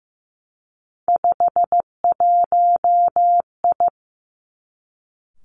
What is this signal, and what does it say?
Morse code, heard as a keyed tone, about 15 wpm, 700 Hz. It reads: 51I
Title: Three Challenging Letter-Number Combinations 15wpm